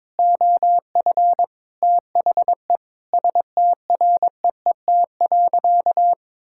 Morse code, heard as a keyed tone, about 22 wpm, 700 Hz. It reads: OF THE STREET.
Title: Sentences from Top 1600 Words 22wpm